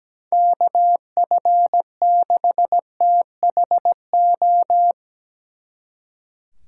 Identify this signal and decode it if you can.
Morse code, heard as a keyed tone, about 18 wpm, 700 Hz. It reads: KF6THO
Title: Speed Racing - USA Call Signs 17wpm